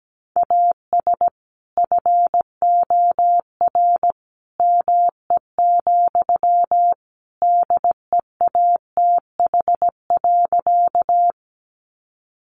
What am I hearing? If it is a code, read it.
Morse code, heard as a keyed tone, about 17 wpm, 700 Hz. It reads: AS FOR ME, DEATH.
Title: Sentences from Top 1100 Words 17wpm